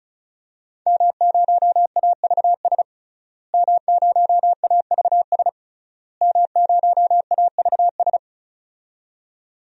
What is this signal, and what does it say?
Morse code, heard as a keyed tone, about 36 wpm, 700 Hz. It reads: M0AVS M0AVS M0AVS
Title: DXCC-100 Call Signs - 5 Characters - Repeated 3x - 35wpm